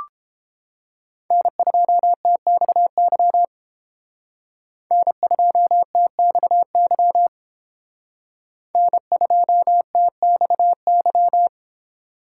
Morse code, heard as a keyed tone, about 31 wpm, 700 Hz. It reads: N2TXY N2TXY N2TXY